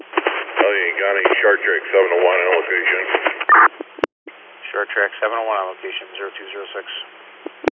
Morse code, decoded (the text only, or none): none